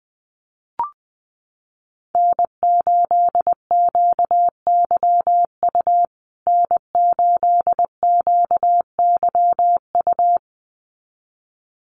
Morse code, heard as a keyed tone, about 20 wpm, 700 Hz. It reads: N8QYU N8QYU
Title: USA Call Signs Repeated 2x 20wpm